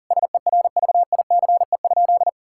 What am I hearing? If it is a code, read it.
Morse code, heard as a keyed tone, about 40 wpm, 700 Hz. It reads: SERVICE?
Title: Sentences from Top 1000 Words - Review Part 5 - 40wpm